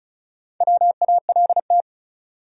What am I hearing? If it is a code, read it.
Morse code, heard as a keyed tone, about 36 wpm, 700 Hz. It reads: WALT
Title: USA Common Names - Max 4 Letters - 35wpm